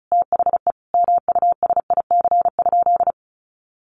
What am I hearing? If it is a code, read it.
Morse code, heard as a keyed tone, about 34 wpm, 700 Hz. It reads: THE MUSIC?
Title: Sentences from Top 300 Words 35wpm